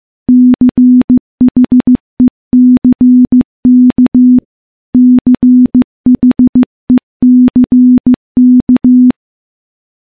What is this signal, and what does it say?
Morse code, heard as a keyed tone, about 15 wpm, 250 Hz. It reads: CHECK CHECK
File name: morse-training-15WPM